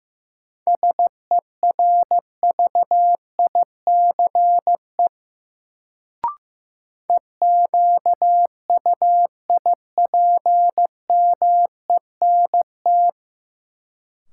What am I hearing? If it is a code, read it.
Morse code, heard as a keyed tone, about 15 wpm, 700 Hz. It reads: SERVICE EQUIPMENT
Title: Sentences from Top 700 Words - New Word Drill - 15wpm